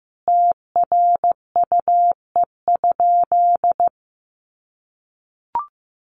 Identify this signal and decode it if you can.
Morse code, heard as a keyed tone, about 15 wpm, 700 Hz. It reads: TRUE?
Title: Sentences from Top 1200 Words 15wpm